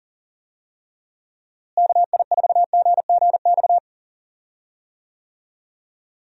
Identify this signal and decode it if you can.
Morse code, heard as a keyed tone, about 37 wpm, 700 Hz. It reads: KI4GGX
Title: USA Call Signs - 6 Characters - Repeated 3x - 40wpm